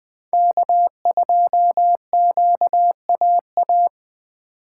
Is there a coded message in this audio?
Yes: Morse code from 0.3 to 3.9 s, heard as a keyed tone, about 20 wpm, 700 Hz. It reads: K2QAA